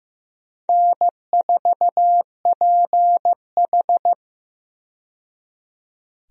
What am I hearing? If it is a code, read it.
Morse code, heard as a keyed tone, about 15 wpm, 700 Hz. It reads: N4PH
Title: USA Call Signs - 4 Characters - 15wpm